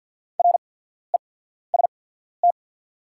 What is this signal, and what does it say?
Morse code, heard as a keyed tone, about 48 wpm, spaced as for 14 wpm, 690 Hz. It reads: REST
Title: 8x Character Spacing - Top 500 Words - 50wpm